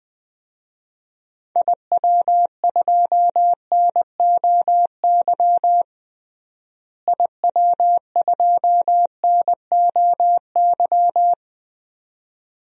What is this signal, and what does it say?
Morse code, heard as a keyed tone, about 20 wpm, 700 Hz. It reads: IW2NOY IW2NOY
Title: DXCC-100 Call Signs - 6 Characters - Repeated 2x - 20wpm